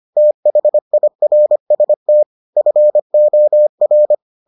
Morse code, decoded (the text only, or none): THIRST FOR